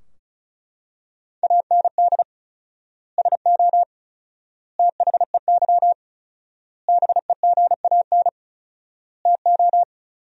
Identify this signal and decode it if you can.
Morse code, heard as a keyed tone, about 34 wpm, 700 Hz. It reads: AND SO THEY BEGAN TO
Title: 4x Word-Spacing - Sentences from Top 400 Words - 35wpm